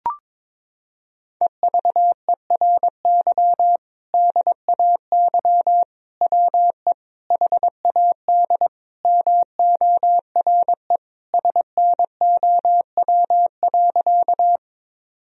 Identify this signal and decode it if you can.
Morse code, heard as a keyed tone, about 21 wpm, 700 Hz. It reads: EVERY DAY WE HAD MORE SNOW.
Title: Sentences from Top 1000 Words - Review Part 3 - 22wpm